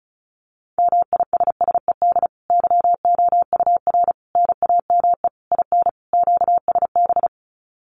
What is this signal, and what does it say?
Morse code, heard as a keyed tone, about 34 wpm, 700 Hz. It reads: MISSED YOUR NAME IN QSB
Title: Common QSO Phrases 35wpm